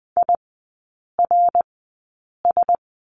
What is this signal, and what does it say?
Morse code, heard as a keyed tone, about 19 wpm, spaced as for 8 wpm, 690 Hz. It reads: IRS
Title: 087 Lesson 15e - Number 5 - Three Characters 20wpm